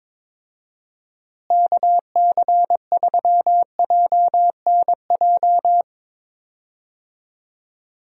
Morse code, heard as a keyed tone, about 22 wpm, 700 Hz. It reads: KC3JNJ
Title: USA Call Signs - 6 Characters - Repeated 2x - 22wpm